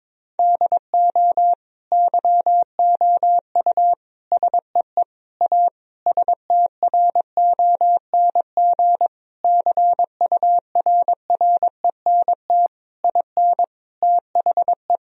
Morse code, heard as a keyed tone, about 22 wpm, 700 Hz. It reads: DO YOU SEE A STRONG CURRENT IN THE
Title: Sentences from Top 1000 Words - Review Part 4 - 22wpm